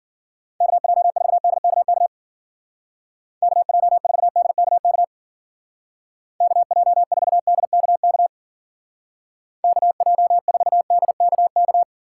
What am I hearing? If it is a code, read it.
Morse code, heard as a keyed tone, about 50 wpm, 700 Hz. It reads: KJ4DKK KJ4DKK KJ4DKK KJ4DKK